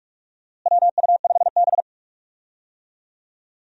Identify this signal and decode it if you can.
Morse code, heard as a keyed tone, about 42 wpm, 700 Hz. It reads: WU5B
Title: USA Call Signs - 4 Characters - Repeated 2x - 45wpm